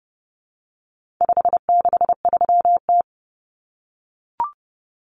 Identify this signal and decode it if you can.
Morse code, heard as a keyed tone, about 30 wpm, 700 Hz. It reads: 563T
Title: Common Cut Numbers - 4-Digit 30wpm